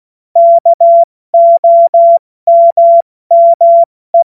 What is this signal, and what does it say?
Morse code, heard as a keyed tone, about 16 wpm, 680 Hz. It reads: KOMME
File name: Morsekurs / Witze / Witz 29. Juni 2021, 70 F80